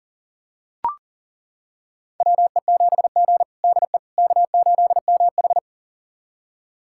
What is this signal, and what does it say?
Morse code, heard as a keyed tone, about 40 wpm, 700 Hz. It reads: WE7G DE K8MH
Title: USA Callsign DE Callsign - 4 Characters - 40wpm